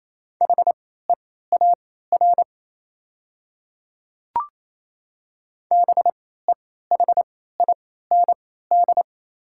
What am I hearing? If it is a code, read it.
Morse code, heard as a keyed tone, about 28 wpm, spaced as for 16 wpm, 690 Hz. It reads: HEAR BEHIND
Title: 3x Character Spacing - Top 500 Words - 28wpm